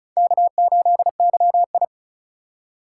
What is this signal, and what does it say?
Morse code, heard as a keyed tone, about 34 wpm, 700 Hz. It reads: K8YI
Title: USA Call Signs - 4x Character Spacing - 35wpm